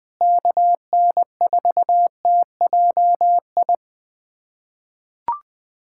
Morse code, heard as a keyed tone, about 20 wpm, 700 Hz. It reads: KN4TJI
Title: USA Callsign DE Callsign 20wpm